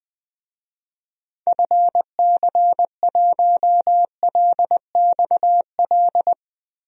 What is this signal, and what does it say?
Morse code, heard as a keyed tone, about 20 wpm, 700 Hz. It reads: FC1LXL
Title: DXCC-100 Call Signs - 6 Characters - 20wpm